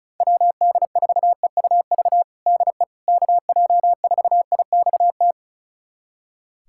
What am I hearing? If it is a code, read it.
Morse code, heard as a keyed tone, about 36 wpm, 700 Hz. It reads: WD4EUV DE KJ4IXT
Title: USA Callsign DE Callsign - 6 Characters - 35wpm